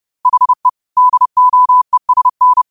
Morse code, heard as a keyed tone, about 30 wpm, 1000 Hz. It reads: HE DOESN